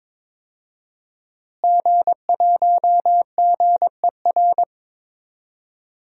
Morse code, heard as a keyed tone, about 22 wpm, 700 Hz. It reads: G1GER
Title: DXCC-100 Call Signs - 5 Characters - Repeated 3x - 22wpm